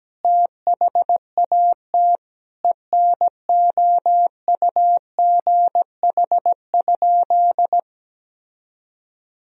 Morse code, heard as a keyed tone, about 17 wpm, 700 Hz. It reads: THAT ENOUGH?
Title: Sentences from Top 1000 Words - Review Part 2 - 17wpm